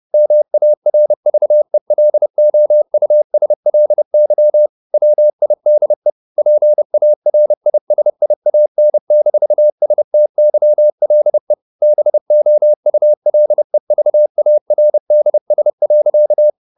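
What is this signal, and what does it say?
Morse code, heard as a keyed tone, about 30 wpm, 600 Hz. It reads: MARVELOUSLY WIDE PARISIAN-STYLE BOULEVARDS.
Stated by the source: Morse code at 30 wpm, the novel Never by Ken Follett in English, from CW med Gnister.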